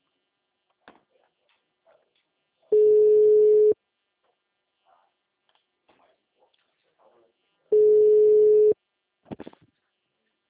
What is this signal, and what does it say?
Ringback tone (Europe, 425 Hz).